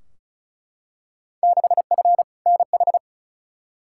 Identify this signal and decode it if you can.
Morse code, heard as a keyed tone, about 36 wpm, 700 Hz. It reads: 6F NH
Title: Field Day Exchanges Encore - 35wpm